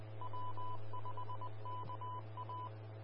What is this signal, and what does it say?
Morse code, heard as a keyed tone, about 20 wpm, 1000 Hz. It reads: W5KA
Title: Austin, TX Amateur Repeaters